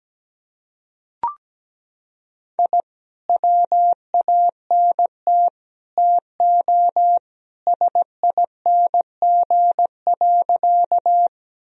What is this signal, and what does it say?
Morse code, heard as a keyed tone, about 17 wpm, 700 Hz. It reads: I WANT TO SING.
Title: Sentences from Top 1000 Words - Review Part 2 - 17wpm